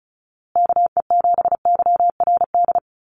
Morse code, heard as a keyed tone, about 34 wpm, 700 Hz. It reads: KE7YRD
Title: USA Call Signs Repeated 2x 35wpm